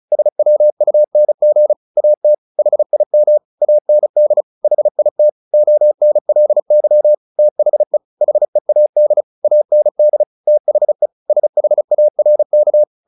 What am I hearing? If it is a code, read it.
Morse code, heard as a keyed tone, about 34 wpm, 600 Hz. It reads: SWUNG AT HIM AND HIT ONLY THE HEAD AND THE SHARK
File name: the old man and the sea 2 35wpm